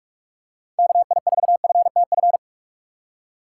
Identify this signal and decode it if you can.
Morse code, heard as a keyed tone, about 45 wpm, 700 Hz. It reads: KI4FTF
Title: USA Call Signs - 6 Characters - 3x Character Spacing - 45wpm